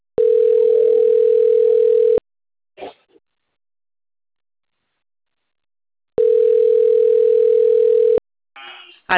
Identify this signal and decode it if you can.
Ringback tone (North America), 440 + 480 Hz.
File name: rg-1600-unknown- - - .63302